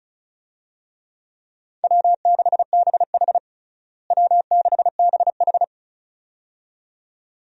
Morse code, heard as a keyed tone, about 36 wpm, 700 Hz. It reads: W6BH W6BH
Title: USA Call Signs - 4 Characters - Repeated 2x - 35wpm